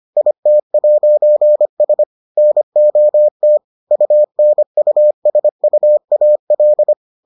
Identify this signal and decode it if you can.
Morse code, heard as a keyed tone, about 25 wpm, 600 Hz. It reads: IT'S NOT UNUSUAL